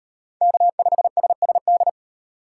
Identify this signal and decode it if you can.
Morse code, heard as a keyed tone, about 35 wpm, 700 Hz. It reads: K5SSD